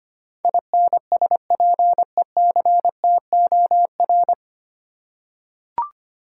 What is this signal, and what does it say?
Morse code, heard as a keyed tone, about 25 wpm, 700 Hz. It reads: INSPECTOR
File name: Nine Letter Words - 25wpm